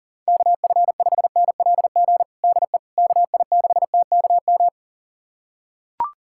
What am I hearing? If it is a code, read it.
Morse code, heard as a keyed tone, about 37 wpm, 700 Hz. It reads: KF5NLG DE KI6TKM
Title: USA Callsign DE Callsign - 6 Characters - 40wpm